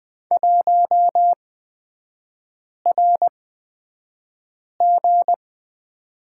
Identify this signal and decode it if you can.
Morse code, heard as a keyed tone, about 20 wpm, spaced as for 5 wpm, 700 Hz. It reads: 1RG
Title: 236 Lesson 38e - Letter q - Three Characters 20wpm